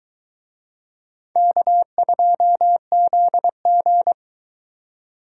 Morse code, heard as a keyed tone, about 21 wpm, 700 Hz. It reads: K2ZG